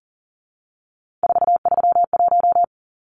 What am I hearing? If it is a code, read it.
Morse code, heard as a keyed tone, about 40 wpm, 700 Hz. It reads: 431
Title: Numbers - 3-Digit 40wpm